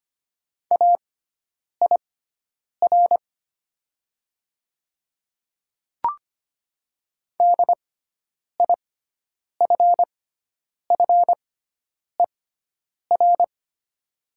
Morse code, heard as a keyed tone, about 25 wpm, spaced as for 9 wpm, 690 Hz. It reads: AIR DIFFER
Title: 6x Character Spacing - Top 200 Words - 25wpm